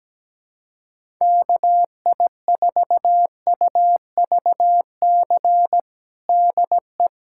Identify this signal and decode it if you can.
Morse code, heard as a keyed tone, about 17 wpm, 700 Hz. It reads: KI4UVC DE
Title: USA Callsign DE Callsign - 6 Characters - 17wpm